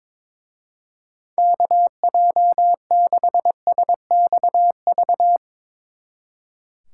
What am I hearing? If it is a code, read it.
Morse code, heard as a keyed tone, about 24 wpm, 700 Hz. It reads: KJ6SXV
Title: Speed Racing - USA Call Signs 22wpm